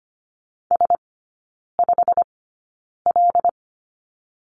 Morse code, heard as a keyed tone, about 25 wpm, spaced as for 9 wpm, 700 Hz. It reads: S5L
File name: course / 25:10wpm / 093 Lesson 16e - Letter c - Three Characters 25wpm